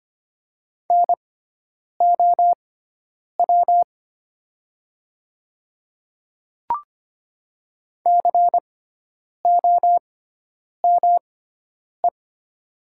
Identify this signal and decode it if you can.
Morse code, heard as a keyed tone, about 25 wpm, spaced as for 9 wpm, 700 Hz. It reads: NOW COME
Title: 6x Character Spacing - Top 100 Words - 25wpm